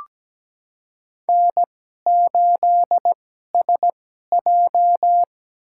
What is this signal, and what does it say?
Morse code, heard as a keyed tone, about 17 wpm, spaced as for 12 wpm, 700 Hz. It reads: N8SJ
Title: USA Call Signs - 4 Characters - 2x Character Spacing - 17wpm